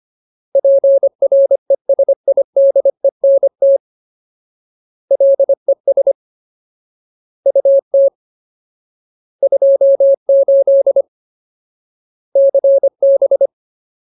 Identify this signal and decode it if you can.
Morse code, heard as a keyed tone, about 25 wpm, 550 Hz. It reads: PRESIDENT LES UT 28 CB
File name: Ultimate Word List / 25WPM Ultimate